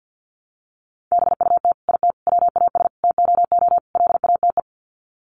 Morse code, heard as a keyed tone, about 50 wpm, 700 Hz. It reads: BUT IT WAS TOO LATE
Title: Sets of 5 Words Encore2 50wpm